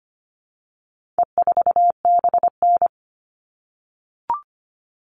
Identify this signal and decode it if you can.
Morse code, heard as a keyed tone, about 25 wpm, 700 Hz. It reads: E4BN
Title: All Cut Numbers - 4-Digit 25wpm